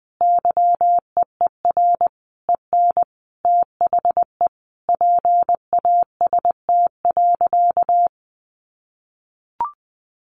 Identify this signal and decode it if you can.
Morse code, heard as a keyed tone, about 20 wpm, 700 Hz. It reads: YEER EN THE PAST.
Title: A Sound Of Thunder 20wpm